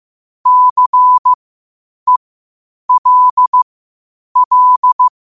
Morse code, heard as a keyed tone, about 15 wpm, spaced as for 8 wpm, 1000 Hz. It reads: CELL
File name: qotdfiles / 8wpm